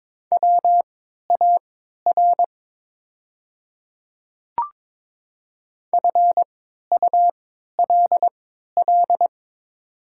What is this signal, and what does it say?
Morse code, heard as a keyed tone, about 22 wpm, spaced as for 13 wpm, 700 Hz. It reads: WAR FULL
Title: 3x Character Spacing - Top 500 Words - 22wpm